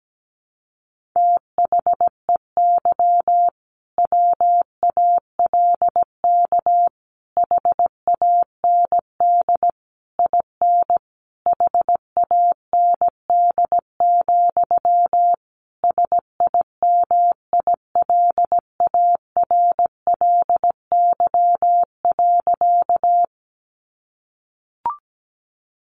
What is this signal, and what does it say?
Morse code, heard as a keyed tone, about 17 wpm, 700 Hz. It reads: THEY WALK HAND IN HAND, SIMILARLY.
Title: Sentences from Top 1400 Words 17wpm